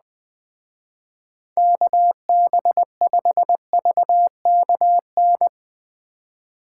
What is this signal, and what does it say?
Morse code, heard as a keyed tone, about 20 wpm, 700 Hz. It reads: KB5VKN
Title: Speed Racing - USA Call Signs 17wpm